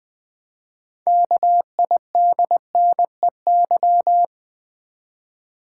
Morse code, heard as a keyed tone, about 20 wpm, 700 Hz. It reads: KIDNEY
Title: 6 Letter Words - First 3 Letters Given - 20wpm